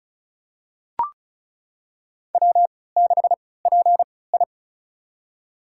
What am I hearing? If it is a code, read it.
Morse code, heard as a keyed tone, about 36 wpm, spaced as for 20 wpm, 700 Hz. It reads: W6PI
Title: USA Call Signs - 4 Characters - 3x Character Spacing - 35wpm